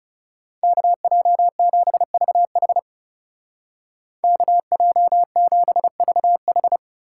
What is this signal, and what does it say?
Morse code, heard as a keyed tone, about 35 wpm, 700 Hz. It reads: KJ7VH KJ7VH
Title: Speed Racing - USA 2x2 Call Signs 30wpm